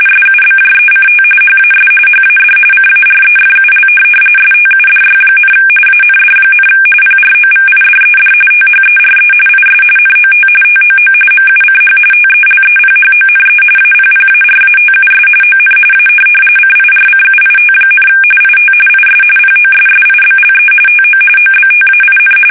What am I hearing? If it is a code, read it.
RTTY, about 45.45 baud. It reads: SUBJECT HAS HAD CAREFUL STUDY, AND, PROVIDED THERE IS NOT TOO MUCH TELEPHONE APPARATUS ON A CIRCUIT, IT IS POSSIBLE TO OBTAIN
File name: fsk850